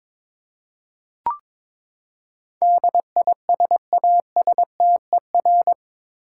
Morse code, heard as a keyed tone, about 22 wpm, 700 Hz. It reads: DISASTER
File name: practice / 22wpm / Eight Letter Words - 22wpm